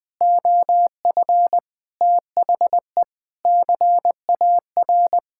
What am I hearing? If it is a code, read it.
Morse code, heard as a keyed tone, about 20 wpm, 700 Hz. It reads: OF THE CAR
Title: Sentences with 4-Word Phrases - 20wpm